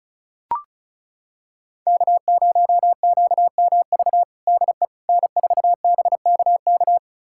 Morse code, heard as a keyed tone, about 36 wpm, 700 Hz. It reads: K0QMV DE N4BKK
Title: USA Callsign DE Callsign - 5 Characters - 35wpm